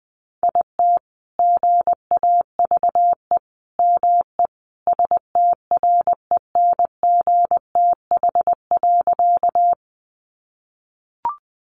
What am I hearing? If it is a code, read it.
Morse code, heard as a keyed tone, about 20 wpm, 700 Hz. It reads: IT GAVE ME STRENGTH.
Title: Sentences from Top 1300 Words 20wpm